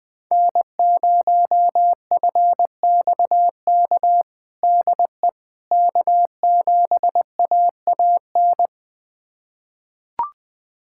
Morse code, heard as a keyed tone, about 20 wpm, 700 Hz. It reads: N0FXK DE K7AAN
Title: USA Callsign DE Callsign - 5 Characters - 20wpm